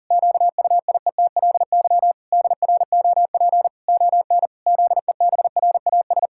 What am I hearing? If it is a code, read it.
Morse code, heard as a keyed tone, about 40 wpm, 700 Hz. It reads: QUIETLY DROP ON ZEBRAS
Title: Pangram Challenge v2 - 40wpm